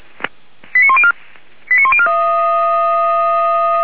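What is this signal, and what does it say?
Five-tone selective call (ZVEI), heard as quick runs of five tones, tones of about 70 ms. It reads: 89104 89104
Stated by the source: ZVEI five-tone call received by SDR, from the feuerwehr-orlamuende.org fire brigade audio archive.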